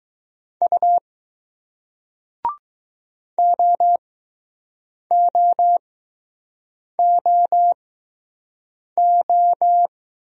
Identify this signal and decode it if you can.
Morse code, heard as a keyed tone, about 20 wpm, 700 Hz. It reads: U O O O O